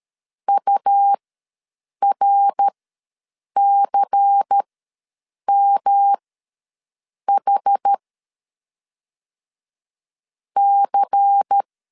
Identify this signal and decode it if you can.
Morse code, heard as a keyed tone, about 13 wpm, spaced as for 7 wpm, 800 Hz. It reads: URCMH C
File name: LEZIONI / 5 LBVC / 5 LBVC ws 12wpm 3159